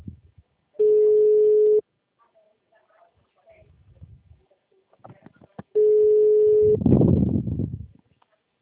Ringback tone (Europe, 425 Hz).